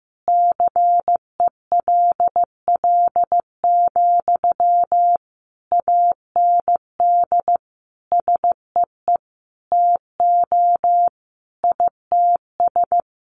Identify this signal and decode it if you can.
Morse code, heard as a keyed tone, about 15 wpm, 700 Hz. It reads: CELL, AND SEE TO ITS
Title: Sentences from Top 1100 Words 15wpm